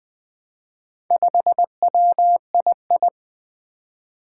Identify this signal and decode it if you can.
Morse code, heard as a keyed tone, about 20 wpm, 700 Hz. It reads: 5WII
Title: Four Challenging Letter-Number Combinations 20wpm